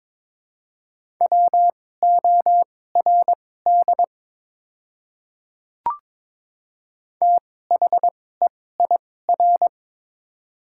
Morse code, heard as a keyed tone, about 22 wpm, spaced as for 16 wpm, 700 Hz. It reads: WORD THEIR
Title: 2x Character Spacing - Top 100 Words - 22wpm